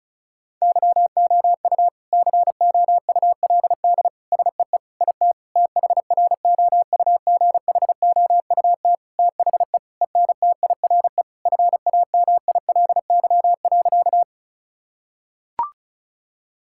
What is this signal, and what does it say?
Morse code, heard as a keyed tone, about 36 wpm, 700 Hz. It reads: YOU COULD SEE IT THROUGHOUT THE ENTIRE FAMILY.
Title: Sentences from Top 1000 Words - Review Part 5 - 35wpm